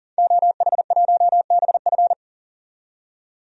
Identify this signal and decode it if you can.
Morse code, heard as a keyed tone, about 37 wpm, 700 Hz. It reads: OH1BF
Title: DXCC-100 Call Signs - 5 Characters - 40wpm